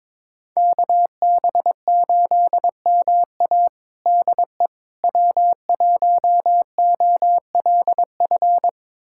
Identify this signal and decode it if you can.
Morse code, heard as a keyed tone, about 22 wpm, 700 Hz. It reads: KB8MA DE W1OLF
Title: USA Callsign DE Callsign - 5 Characters - 22wpm